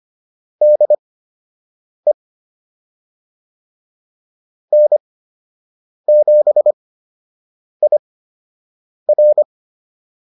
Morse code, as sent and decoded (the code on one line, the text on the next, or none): -.. / -. --... .. .-.
DE N7IR